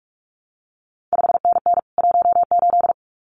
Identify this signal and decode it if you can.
Morse code, heard as a keyed tone, about 46 wpm, 700 Hz. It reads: 5NN 18
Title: CQ WW DX Contest Exchange Encore - 45wpm